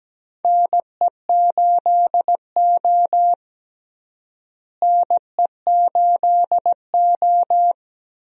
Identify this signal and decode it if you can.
Morse code, heard as a keyed tone, about 17 wpm, 700 Hz. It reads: NE8O NE8O
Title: USA Call Signs - 4 Characters - Repeated 2x - 17wpm